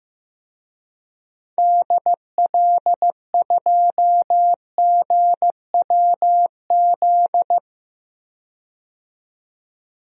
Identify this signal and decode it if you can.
Morse code, heard as a keyed tone, about 15 wpm, 700 Hz. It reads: DL2GWZ
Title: DXCC-100 Call Signs - 6 Characters - 15wpm